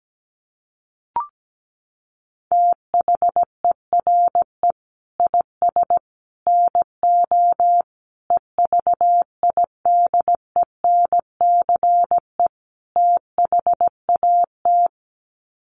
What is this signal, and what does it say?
Morse code, heard as a keyed tone, about 17 wpm, 700 Hz. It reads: THERE IS NO EVIDENCE THAT
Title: Sets of 5 Words Encore 17wpm